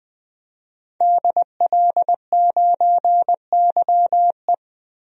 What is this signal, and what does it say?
Morse code, heard as a keyed tone, about 20 wpm, 700 Hz. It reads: DL9YE
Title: DXCC-100 Call Signs - 5 Characters - 20wpm